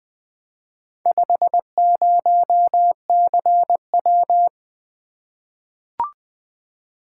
Morse code, heard as a keyed tone, about 20 wpm, 700 Hz. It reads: 50CW
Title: Four Letter-Number Combinations - 20wpm